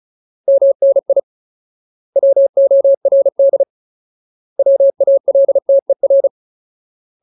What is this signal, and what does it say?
Morse code, heard as a keyed tone, about 34 wpm, 550 Hz. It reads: MNI WORD WALTER
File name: Ultimate Word List / 35WPM Ultimate